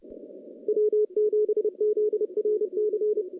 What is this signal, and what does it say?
Morse code, heard as a keyed tone, about 30 wpm, 410 Hz, with radio noise. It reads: W7ZRC
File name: contests / CW-SS-k5tr- -0000